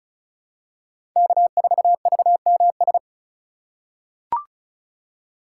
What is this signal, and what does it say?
Morse code, heard as a keyed tone, about 36 wpm, 700 Hz. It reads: K4VMS
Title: USA Call Signs - 5 Characters - 3x Character Spacing - 35wpm